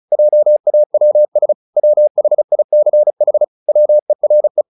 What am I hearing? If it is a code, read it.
Morse code, heard as a keyed tone, about 34 wpm, 600 Hz. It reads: JAWS WHICH WERE
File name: the old man and the sea 2 35wpm